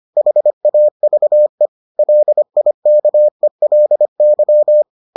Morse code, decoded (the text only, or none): HAVE LIKELY